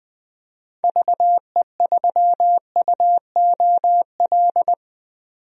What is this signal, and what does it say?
Morse code, heard as a keyed tone, about 21 wpm, 700 Hz. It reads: VE3UOL
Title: DXCC-100 Call Signs - 6 Characters - 20wpm